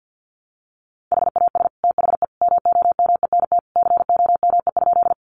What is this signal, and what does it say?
Morse code, heard as a keyed tone, about 50 wpm, 700 Hz. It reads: HAS THE MOMENT COME?
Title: Sentences from Top 1000 Words 50wpm